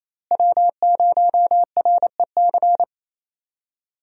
Morse code, heard as a keyed tone, about 27 wpm, 700 Hz. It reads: W0REC
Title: USA Call Signs - 5 Characters - Repeated 3x - 28wpm